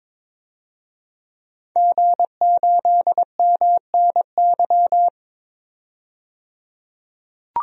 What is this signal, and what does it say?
Morse code, heard as a keyed tone, about 22 wpm, 700 Hz. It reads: G8MNY
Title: DXCC-100 Call Signs - 5 Characters - 22wpm